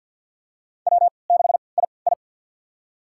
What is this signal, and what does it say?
Morse code, heard as a keyed tone, about 51 wpm, spaced as for 28 wpm, 700 Hz. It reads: W6II
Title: USA Call Signs - 4 Characters - 3x Character Spacing - 50wpm